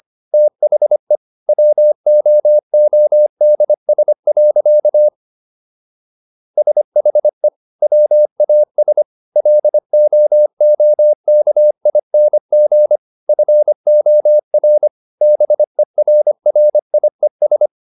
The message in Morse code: THE WOODS. SHE WAS LOOKING FOR BERRIES

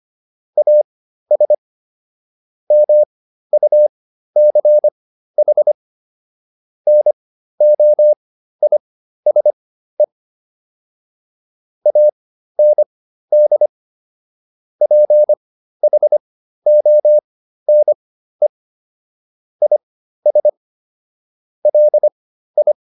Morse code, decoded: AS MUCH NOISE AND PHONE IS LI

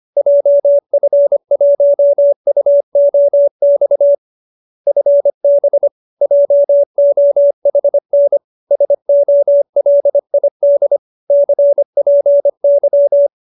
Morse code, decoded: JF1UOX FB JOHN SOLID CPY

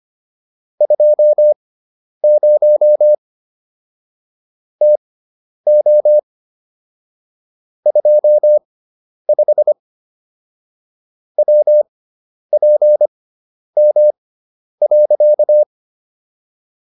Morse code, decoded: 20 TO 25 WPM.

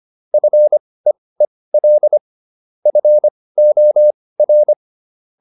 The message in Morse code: FEEL FOR

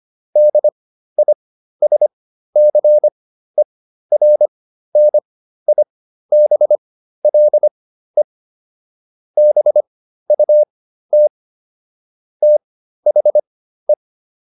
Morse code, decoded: DISCERNIBLE BUT THE

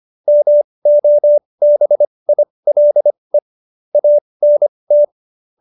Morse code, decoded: MOBILE ANT

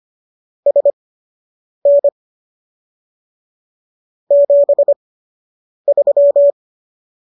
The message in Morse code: SN 73

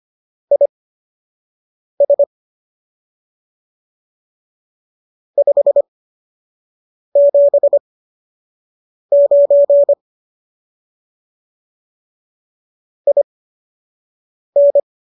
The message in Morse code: IS 579 IN

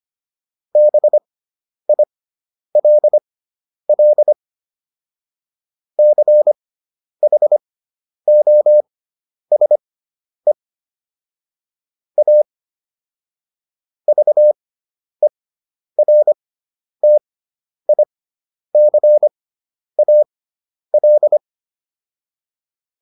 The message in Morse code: BILL CHOSE A VERTICAL

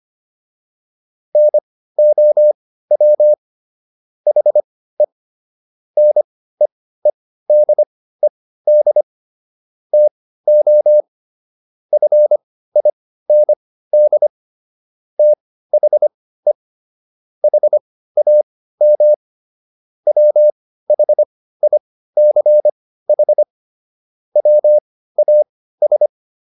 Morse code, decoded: NOW HE NEEDED TO FIND THE HAM WHICH WAS

